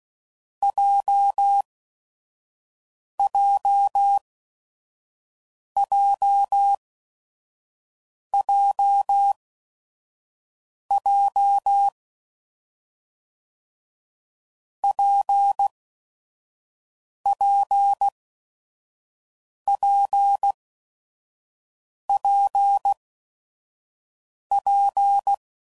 Morse code, heard as a keyed tone, about 16 wpm, spaced as for 5 wpm, 800 Hz. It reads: JJJJJ PPPPP
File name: Combinations / Difficult Combos / WPJ WPJ1